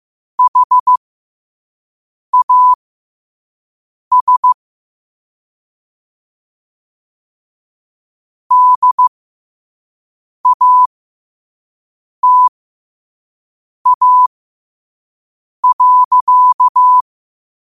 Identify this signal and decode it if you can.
Morse code, heard as a keyed tone, about 15 wpm, spaced as for 5 wpm, 1000 Hz. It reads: HAS DATA.
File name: qotdfiles / 5wpm